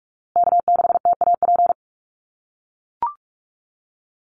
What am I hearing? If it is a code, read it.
Morse code, heard as a keyed tone, about 44 wpm, 700 Hz. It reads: K6TAP
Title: USA Call Signs Repeated 2x 45wpm